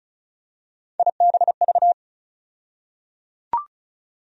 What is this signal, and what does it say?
Morse code, heard as a keyed tone, about 36 wpm, 700 Hz. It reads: IBV
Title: USA Call Sign Postfixes - 3 Letters 35wpm